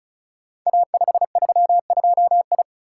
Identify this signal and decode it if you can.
Morse code, heard as a keyed tone, about 36 wpm, 700 Hz. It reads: A532I
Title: Code Groups - 5 Letter:Numbers - 35wpm